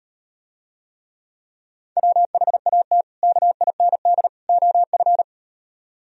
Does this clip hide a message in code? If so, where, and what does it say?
Yes: Morse code from 2.0 to 5.2 s, heard as a keyed tone, about 38 wpm, 700 Hz. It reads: WHAT KIND OF